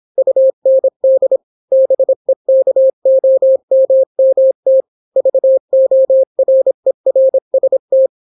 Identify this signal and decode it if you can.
Morse code, heard as a keyed tone, about 26 wpm, 530 Hz. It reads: UND BEKOMMT VORERST